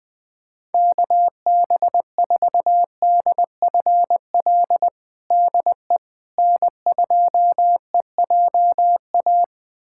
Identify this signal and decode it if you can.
Morse code, heard as a keyed tone, about 21 wpm, 700 Hz. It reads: KB4DFL DE N2EJA
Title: USA Callsign DE Callsign 20wpm